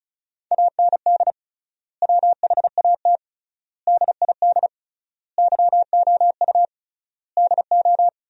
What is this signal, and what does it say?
Morse code, heard as a keyed tone, about 36 wpm, 700 Hz. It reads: AND WHAT DID YOU DO
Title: 3x Word-Spacing - Sentences from Top 100 Words - 35wpm